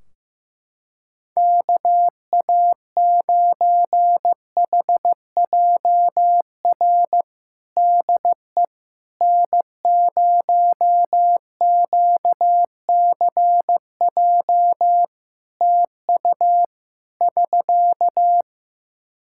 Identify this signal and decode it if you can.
Morse code, heard as a keyed tone, about 15 wpm, 700 Hz. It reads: KA9HJR DE N0QCJ TU <SK>
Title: Practice QSO - Exchange 4 - Ending - 15wpm